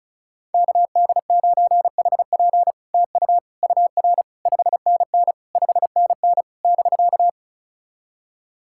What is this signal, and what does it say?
Morse code, heard as a keyed tone, about 36 wpm, 700 Hz. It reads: KD9HP TU UR 5NN 5NN <BK>
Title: SOTA QSO Practice 35wpm